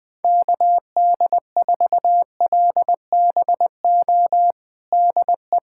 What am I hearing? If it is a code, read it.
Morse code, heard as a keyed tone, about 20 wpm, 700 Hz. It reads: KD4LBO DE
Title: USA Callsign DE Callsign - 6 Characters - 20wpm